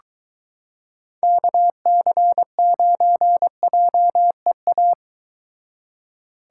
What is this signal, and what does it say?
Morse code, heard as a keyed tone, about 23 wpm, 700 Hz. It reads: KC9JEA